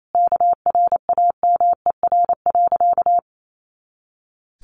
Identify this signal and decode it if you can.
Morse code, heard as a keyed tone, about 28 wpm, 700 Hz. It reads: KRAMER.
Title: A Sound Of Thunder 28wpm